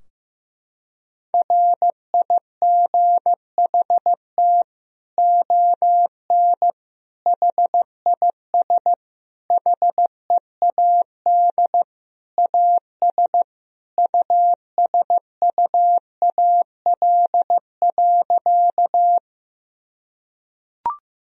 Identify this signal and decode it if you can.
Morse code, heard as a keyed tone, about 15 wpm, 700 Hz. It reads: RIGHT ON HIS HEAD AS USUAL.
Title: Sentences from Top 1000 Words - Review Part 2 - 15wpm